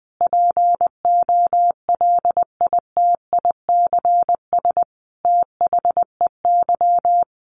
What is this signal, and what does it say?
Morse code, heard as a keyed tone, about 20 wpm, 700 Hz. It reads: POLITICS THEY
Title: Sentences from Top 1000 Words 20wpm